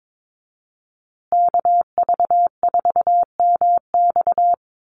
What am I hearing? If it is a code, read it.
Morse code, heard as a keyed tone, about 22 wpm, 700 Hz. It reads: KV4MX
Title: USA Call Signs Repeated 3x 22wpm